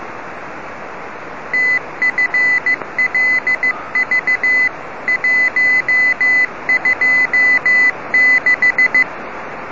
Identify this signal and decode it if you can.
Morse code, heard as a keyed tone, about 15 wpm, 2000 Hz. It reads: TFLV 126
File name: rs-21 1515utc